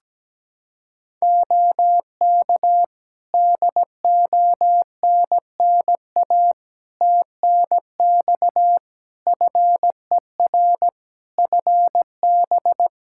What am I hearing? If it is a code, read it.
Morse code, heard as a keyed tone, about 17 wpm, 700 Hz. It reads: OK DONNA TNX FER FB
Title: Practice QSO - Exchange 4 - Ending - 17wpm